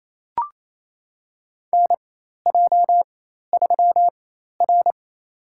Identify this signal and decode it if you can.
Morse code, heard as a keyed tone, about 28 wpm, spaced as for 13 wpm, 700 Hz. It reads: NJ3R
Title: USA Call Signs - 4 Characters - 4x Character Spacing - 28wpm